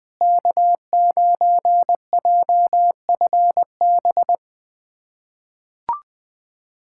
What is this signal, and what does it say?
Morse code, heard as a keyed tone, about 20 wpm, 700 Hz. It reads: K9JFB
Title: POTA QSO Practice - 20wpm